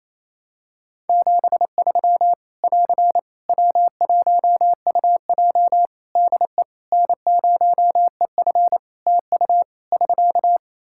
Morse code, heard as a keyed tone, about 28 wpm, 700 Hz. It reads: 73 <AR> W1UJ DE N0EF TU <SK>
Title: Practice QSO - Exchange 4 - Ending - 28wpm